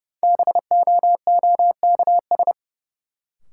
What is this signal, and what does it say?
Morse code, heard as a keyed tone, about 32 wpm, 700 Hz. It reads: BOOKS
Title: Sentences from Top 1000 Words - Review Part 4 - New Word Drill - 30wpm